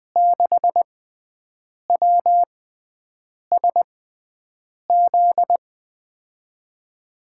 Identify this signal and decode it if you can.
Morse code, heard as a keyed tone, about 20 wpm, spaced as for 7 wpm, 700 Hz. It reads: 6WSZ
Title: USA Call Signs - 6 Characters - 6x Character Spacing - 20wpm